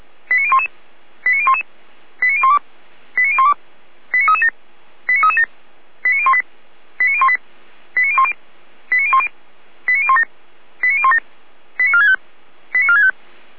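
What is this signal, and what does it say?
Five-tone selective call (ZVEI), heard as quick runs of five tones, tones of about 70 ms. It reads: 89011 89011 89012 89012 89337 89337 89018 89018 89010 89010 89017 89017 89465 89465
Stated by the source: ZVEI five-tone selective call received via SDR.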